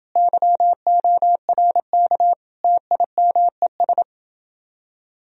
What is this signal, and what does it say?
Morse code, heard as a keyed tone, about 30 wpm, 700 Hz. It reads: YORK TIMES